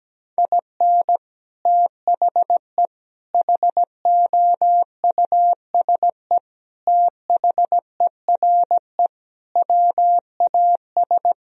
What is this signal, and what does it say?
Morse code, heard as a keyed tone, about 17 wpm, 700 Hz. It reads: IN THE HOUSE THERE WAS
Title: Sentences from Top 1000 Words - Review Part 2 - 17wpm